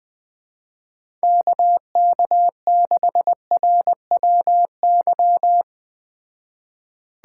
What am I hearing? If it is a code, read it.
Morse code, heard as a keyed tone, about 21 wpm, 700 Hz. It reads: KK6RWY